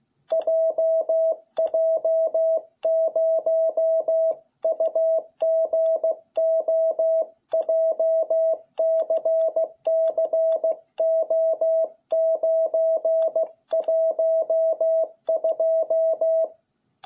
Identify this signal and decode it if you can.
Morse code, heard as a keyed tone, about 16 wpm, 660 Hz. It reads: JJ0UGO JCCO912